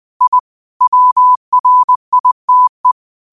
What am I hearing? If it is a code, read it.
Morse code, heard as a keyed tone, about 20 wpm, 1000 Hz. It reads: I WRITE
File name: qotdfiles / 20wpm